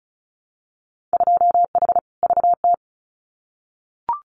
Morse code, heard as a keyed tone, about 34 wpm, 700 Hz. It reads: 2H VT
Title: Winter Field Day Exchanges 35wpm